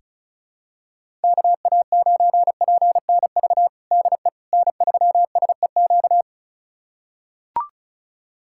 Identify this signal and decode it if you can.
Morse code, heard as a keyed tone, about 36 wpm, 700 Hz. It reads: KA9PNV DE N3SEQ
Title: USA Callsign DE Callsign 35wpm